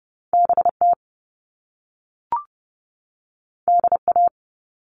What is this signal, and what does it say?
Morse code, heard as a keyed tone, about 30 wpm, 700 Hz. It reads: B T D A
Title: All Cut Numbers - 2-Digit 30wpm